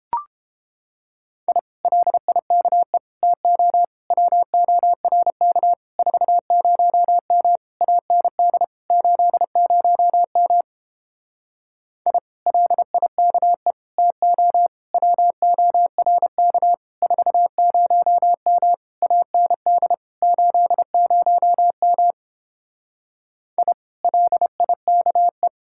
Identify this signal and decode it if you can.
Morse code, heard as a keyed tone, about 31 wpm, 700 Hz. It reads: I LIKE TO WORK 40M AND 80M I LIKE TO WORK 40M AND 80M I LIKE